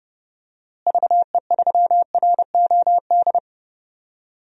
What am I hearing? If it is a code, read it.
Morse code, heard as a keyed tone, about 30 wpm, 700 Hz. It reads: VE3ROD
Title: DXCC-100 Call Signs - 6 Characters - Repeated 3x - 30wpm